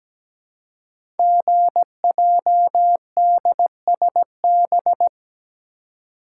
Morse code, heard as a keyed tone, about 17 wpm, 700 Hz. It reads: GJDSB
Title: Code Groups - 5 Letters - 17wpm